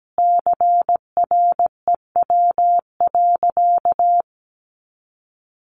Morse code, heard as a keyed tone, about 17 wpm, 700 Hz. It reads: CREW.
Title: Sentences from Top 1500 Words 17wpm